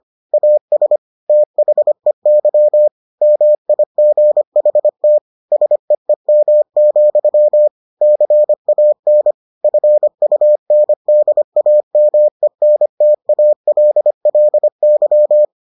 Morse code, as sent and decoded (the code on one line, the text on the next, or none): .- ... / - .... -.-- / -- .. --. .... - / ... . -- --..-- / -.-. .- -. / ..-. ..- -. -.. .- -- . -. - .- .-.. .-.. -.--
AS THEY MIGHT SEEM, CAN FUNDAMENTALLY